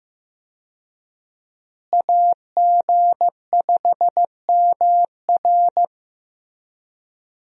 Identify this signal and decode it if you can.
Morse code, heard as a keyed tone, about 15 wpm, 700 Hz. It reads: AG5MR
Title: USA Call Signs - 5 Characters - Repeated 3x - 15wpm